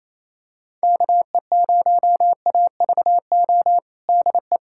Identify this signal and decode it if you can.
Morse code, heard as a keyed tone, about 27 wpm, 700 Hz. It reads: KE0AVO DE
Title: USA Callsign DE Callsign 28wpm